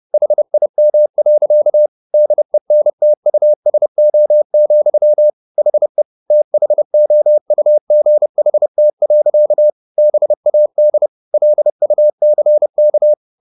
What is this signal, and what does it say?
Morse code, heard as a keyed tone, about 30 wpm, 600 Hz. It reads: HIM. DENTUSO, HE THOUGHT. BAD LUCK